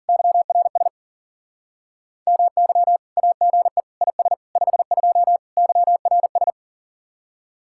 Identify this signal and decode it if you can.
Morse code, heard as a keyed tone, about 50 wpm, 700 Hz. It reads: YRS MY AGE IS 52 YRS